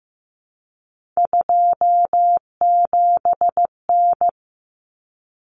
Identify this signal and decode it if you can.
Morse code, heard as a keyed tone, about 15 wpm, 700 Hz. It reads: 27N
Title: Common Cut Numbers - 3-Digit 15wpm